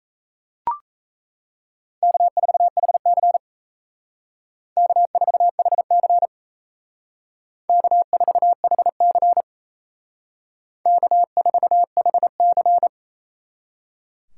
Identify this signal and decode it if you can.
Morse code, heard as a keyed tone, about 34 wpm, 700 Hz. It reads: K4HC K4HC K4HC K4HC
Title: Speed Racing - USA 1x2 Call Signs 28wpm